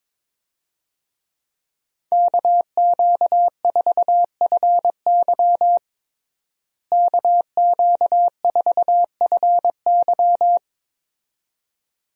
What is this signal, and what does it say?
Morse code, heard as a keyed tone, about 22 wpm, 700 Hz. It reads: KQ4FY KQ4FY
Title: USA Call Signs - 5 Characters - Repeated 2x - 22wpm